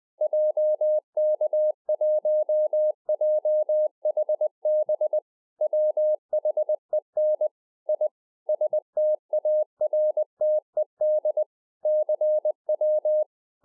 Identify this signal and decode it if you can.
Morse code, heard as a keyed tone, about 20 wpm, 610 Hz. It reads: JK1JHB WHEN I STARTED CW